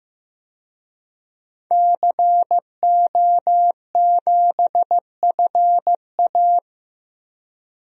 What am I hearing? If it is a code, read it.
Morse code, heard as a keyed tone, about 15 wpm, 700 Hz. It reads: CO7FA
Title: DXCC-100 Call Signs - 5 Characters - 15wpm